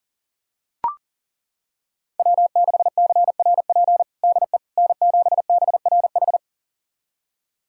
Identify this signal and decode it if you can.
Morse code, heard as a keyed tone, about 40 wpm, 700 Hz. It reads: W6CRP DE N7BRH
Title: USA Callsign DE Callsign - 5 Characters - 40wpm